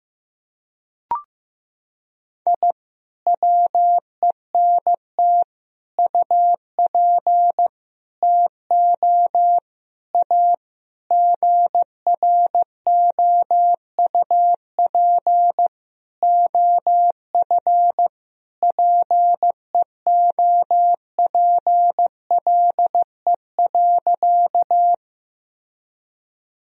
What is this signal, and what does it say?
Morse code, heard as a keyed tone, about 15 wpm, 700 Hz. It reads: I WENT UP TO A GROUP OF PEOPLE.
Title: Sentences from Top 1000 Words - Review Part 1 - 15wpm